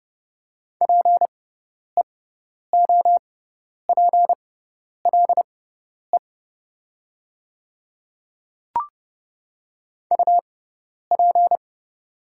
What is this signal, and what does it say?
Morse code, heard as a keyed tone, about 30 wpm, spaced as for 10 wpm, 700 Hz. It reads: PEOPLE UP
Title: 6x Character Spacing - Top 100 Words - 30wpm